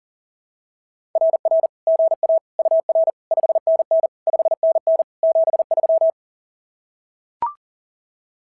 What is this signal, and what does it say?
Morse code, heard as a keyed tone, about 40 wpm, 660 Hz. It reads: RR GA UR 5NN 5NN 73
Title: SOTA QSO Practice 40wpm